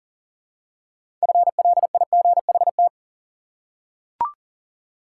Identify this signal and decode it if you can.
Morse code, heard as a keyed tone, about 40 wpm, 700 Hz. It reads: FLIGHT
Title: Six Letter Words - 40wpm